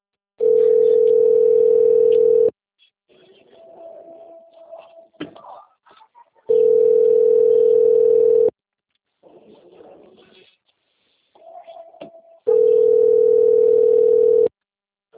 Ringback tone (North America), 440 + 480 Hz.